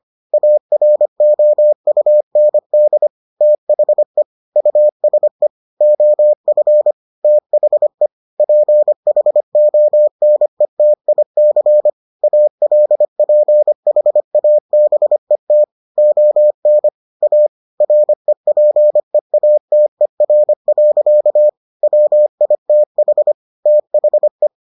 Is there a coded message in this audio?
Yes: Morse code from 0.3 to 24.5 s, heard as a keyed tone, about 25 wpm, 600 Hz. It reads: AROUND THE USE OF THE PHONETIC ALPHABET ON A REPEATER. WITH THE